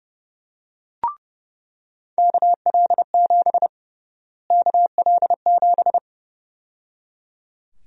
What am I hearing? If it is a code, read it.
Morse code, heard as a keyed tone, about 32 wpm, 700 Hz. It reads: KL7 KL7
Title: USA Call Sign Prefixes - Repeated 2x - 30wpm